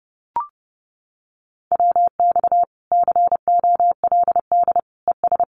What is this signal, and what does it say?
Morse code, heard as a keyed tone, about 30 wpm, 700 Hz. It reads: WX COLD ES
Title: Common QSO Phrases 30wpm